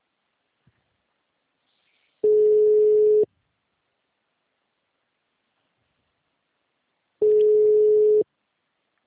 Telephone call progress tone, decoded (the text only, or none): ringback tone